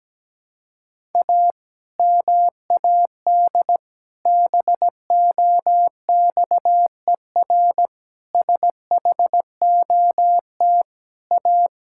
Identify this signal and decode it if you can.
Morse code, heard as a keyed tone, about 17 wpm, 700 Hz. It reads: A MAD BOXER SHOT A
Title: Pangram Challenge v2 - 17wpm